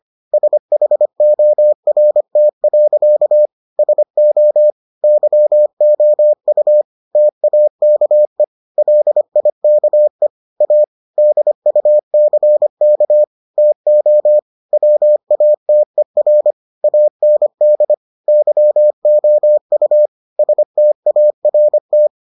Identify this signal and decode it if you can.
Morse code, heard as a keyed tone, about 25 wpm, 600 Hz. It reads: SHORT. SO YOU TAKE LIKE A DUCK TO WATER AND YOU START